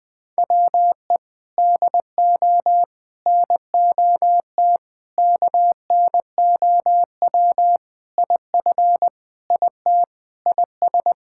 Morse code, heard as a keyed tone, about 21 wpm, 700 Hz. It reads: WE DO NOT KNOW IF IT IS
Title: Sentences with 5-Word Phrases - Spoken Beginning - 20wpm